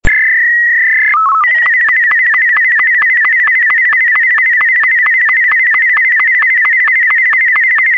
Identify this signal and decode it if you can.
SSTV (slow-scan television), Martin 2.
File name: sstv / sstv1 / martin m2.48k